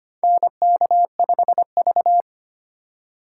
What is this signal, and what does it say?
Morse code, heard as a keyed tone, about 25 wpm, 700 Hz. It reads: NK5V
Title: USA Call Signs - 4 Characters - 3x Character Spacing - 25wpm